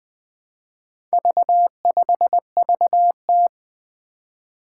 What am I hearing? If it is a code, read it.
Morse code, heard as a keyed tone, about 20 wpm, 700 Hz. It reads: V5VT
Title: Four Challenging Letter-Number Combinations 20wpm